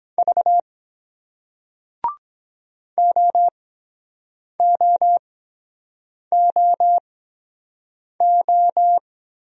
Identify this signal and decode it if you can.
Morse code, heard as a keyed tone, about 23 wpm, spaced as for 7 wpm, 700 Hz. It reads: V OOOO